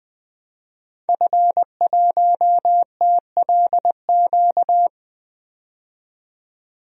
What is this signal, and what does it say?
Morse code, heard as a keyed tone, about 20 wpm, 700 Hz. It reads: F1TLQ
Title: DXCC-100 Call Signs - 5 Characters - 20wpm